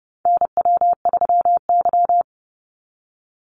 Morse code, heard as a keyed tone, about 29 wpm, 700 Hz. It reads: NW3Y
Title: Call Signs Repeated 2x - 30wpm